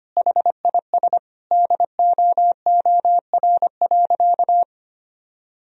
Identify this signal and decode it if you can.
Morse code, heard as a keyed tone, about 25 wpm, 700 Hz. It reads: HIS DOOR.
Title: Sentences from Top 1000 Words - Review Part 2 - 25wpm